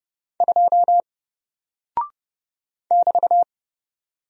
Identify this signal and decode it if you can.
Morse code, heard as a keyed tone, about 30 wpm, 700 Hz. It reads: 2 <BT>
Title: Single Character (Letter:Number:Punctuation:Prosign) 30wpm